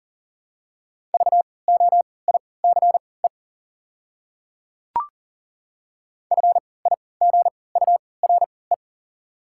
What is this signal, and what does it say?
Morse code, heard as a keyed tone, about 40 wpm, spaced as for 23 wpm, 700 Hz. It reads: VOICE FIGURE
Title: 3x Character Spacing - Top 500 Words - 40wpm